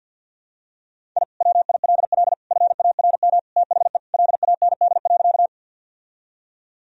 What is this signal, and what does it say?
Morse code, heard as a keyed tone, about 50 wpm, 700 Hz. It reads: I WILL FARM THE LAND.
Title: Sentences from Top 1000 Words - Review Part 2 - 50wpm